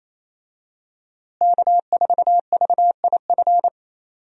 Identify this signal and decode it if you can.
Morse code, heard as a keyed tone, about 28 wpm, 700 Hz. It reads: K4VIF
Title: USA Call Signs - 5 Characters - Repeated 2x - 28wpm